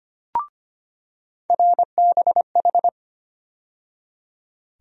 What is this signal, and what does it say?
Morse code, heard as a keyed tone, about 25 wpm, 700 Hz. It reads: RBH
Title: USA Call Sign Postfixes - 3 Letters 25wpm